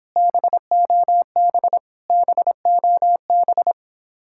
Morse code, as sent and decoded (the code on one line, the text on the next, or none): -... --- -... / -... --- -...
BOB BOB